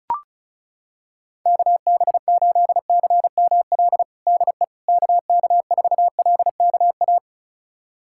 Morse code, heard as a keyed tone, about 36 wpm, 700 Hz. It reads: KB8CML DE KK4LKA
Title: USA Callsign DE Callsign 35wpm